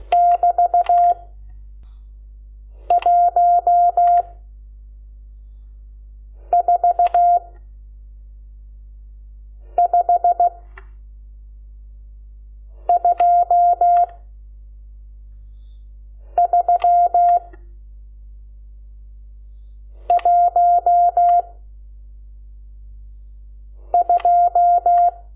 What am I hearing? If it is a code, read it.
Morse code, heard as a keyed tone, about 16 wpm, 700 Hz. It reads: <BT> 1 4 5 2 3 1 2
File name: Lecciones / CLASE Nº12